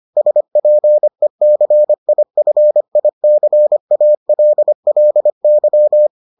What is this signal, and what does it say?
Morse code, heard as a keyed tone, about 25 wpm, 600 Hz. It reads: SPECIFICALLY